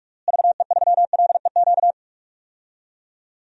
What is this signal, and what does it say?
Morse code, heard as a keyed tone, about 46 wpm, 700 Hz. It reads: VE3LEQ
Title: DXCC-100 Call Signs - 6 Characters - Repeated 3x - 45wpm